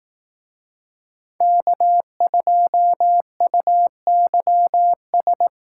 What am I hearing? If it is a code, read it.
Morse code, heard as a keyed tone, about 18 wpm, 700 Hz. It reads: K2UYS
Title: Speed Racing - USA 1x3 Call Signs 15wpm